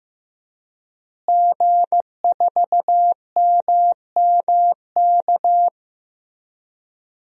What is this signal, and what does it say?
Morse code, heard as a keyed tone, about 15 wpm, 700 Hz. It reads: G4MMK
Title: DXCC-100 Call Signs - 5 Characters - Repeated 3x - 15wpm